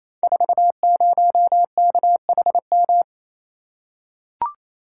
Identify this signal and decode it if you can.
Morse code, heard as a keyed tone, about 28 wpm, 700 Hz. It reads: 40KHM
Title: Code Groups - 5 Letter:Numbers - 28wpm